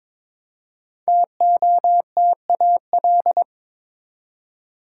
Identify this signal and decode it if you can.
Morse code, heard as a keyed tone, about 22 wpm, 700 Hz. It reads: TOTAL
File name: 5 Letter Words - First 3 Letters Given - 22wpm